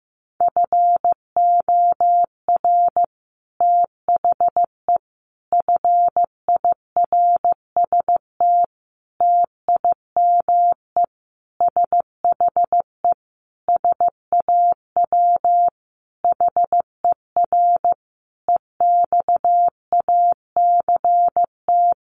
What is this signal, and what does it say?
Morse code, heard as a keyed tone, about 15 wpm, 700 Hz. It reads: FOR THE FIRST TIME SHE SAW HER EXACT
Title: Sentences from Top 1300 Words 15wpm